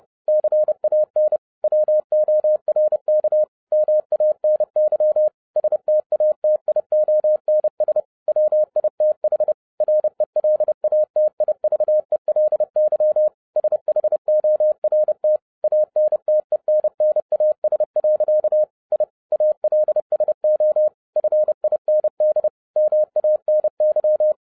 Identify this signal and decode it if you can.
Morse code, heard as a keyed tone, about 30 wpm, 610 Hz. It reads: CAN WORK MANY STATIONS WITH RELATIVELY SHORT ANTENNAS. I ALSO FIND MANY
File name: 2012-01-mp3 / n12-I DIDNT MISS THE SOLAR CYCLE 24-30wpm